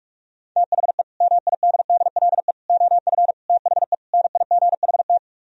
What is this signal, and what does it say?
Morse code, heard as a keyed tone, about 44 wpm, 700 Hz. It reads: THE MIDDLE OF THE NIGHT